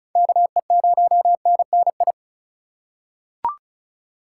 Morse code, heard as a keyed tone, about 36 wpm, 700 Hz. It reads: KE0NNI
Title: USA Call Signs - 6 Characters - 2x Character Spacing - 35wpm